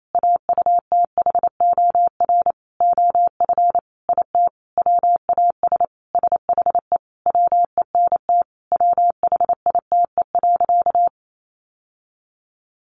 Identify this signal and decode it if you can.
Morse code, heard as a keyed tone, about 28 wpm, 700 Hz. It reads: AUTHOR OF IT WAS SHE WENT WHITE.
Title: Sentences from Top 1200 Words 28wpm